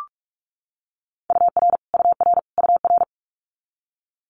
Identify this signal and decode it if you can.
Morse code, heard as a keyed tone, about 46 wpm, 700 Hz. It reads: UR UR UR
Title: Multi-letter Phonemes Repeated 3x 45wpm